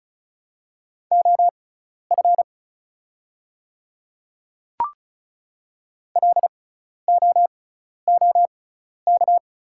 Morse code, heard as a keyed tone, about 36 wpm, spaced as for 12 wpm, 700 Hz. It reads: OF LOOK
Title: 6x Character Spacing - Top 100 Words - 35wpm